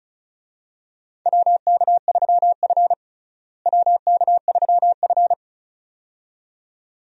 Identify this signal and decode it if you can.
Morse code, heard as a keyed tone, about 36 wpm, 700 Hz. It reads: WK3F WK3F
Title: USA Call Signs - 4 Characters - Repeated 2x - 35wpm